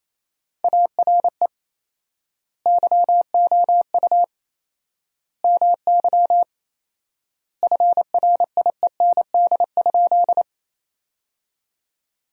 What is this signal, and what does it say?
Morse code, heard as a keyed tone, about 28 wpm, 700 Hz. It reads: ARE YOU MY FRIEND?
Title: 4x Word-Spacing - Sentences from Top 300 Words - 28wpm